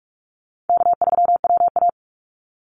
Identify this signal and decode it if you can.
Morse code, heard as a keyed tone, about 46 wpm, 700 Hz. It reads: K3WA
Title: Easy Call Signs - 45wpm